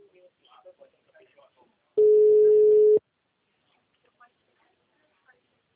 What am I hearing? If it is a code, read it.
Ringback tone (Europe, 425 Hz), with speech.